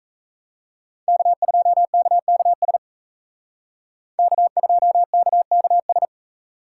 Morse code, heard as a keyed tone, about 35 wpm, 700 Hz. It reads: K2KKS K2KKS